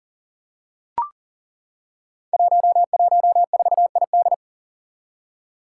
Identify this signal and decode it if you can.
Morse code, heard as a keyed tone, about 37 wpm, 700 Hz. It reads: 114ID
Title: Five Challenging Letter-Number Combinations 40wpm